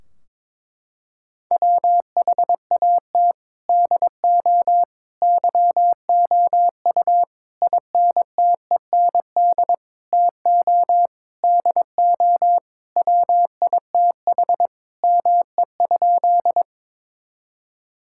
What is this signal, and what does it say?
Morse code, heard as a keyed tone, about 22 wpm, 700 Hz. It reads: WHAT DO YOU INTEND TO DO WITH ME?
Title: Sentences from Top 1600 Words 22wpm